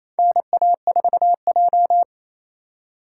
Morse code, heard as a keyed tone, about 28 wpm, 700 Hz. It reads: NA4J
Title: USA Call Signs - 3x Character Spacing - 28wpm